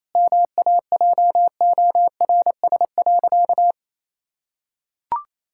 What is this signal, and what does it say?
Morse code, heard as a keyed tone, about 28 wpm, 700 Hz. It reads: MAJORS.
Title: Pangram Challenge v2 - 28wpm